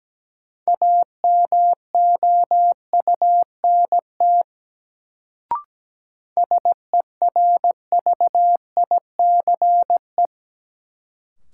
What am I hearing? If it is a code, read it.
Morse code, heard as a keyed tone, about 17 wpm, 700 Hz. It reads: AMOUNT SERVICE
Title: Sentences from Top 700 Words - New Word Drill - 17wpm